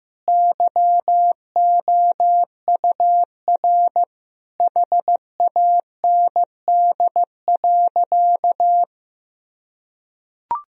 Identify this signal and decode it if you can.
Morse code, heard as a keyed tone, about 15 wpm, 700 Hz. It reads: YOUR HAND.
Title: Sentences from Top 1000 Words - Review Part 2 - 15wpm